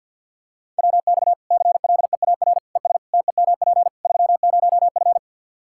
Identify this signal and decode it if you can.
Morse code, heard as a keyed tone, about 52 wpm, 700 Hz. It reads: WX CLEAR ES TEMP 30F
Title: Practice QSO - Exchange 2 - Info - 50wpm